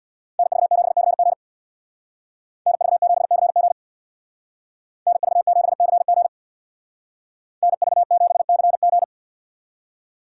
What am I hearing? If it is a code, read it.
Morse code, heard as a keyed tone, about 62 wpm, 700 Hz. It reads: NV7CG NV7CG NV7CG NV7CG
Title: Speed Racing - USA 2x2 Call Signs 50wpm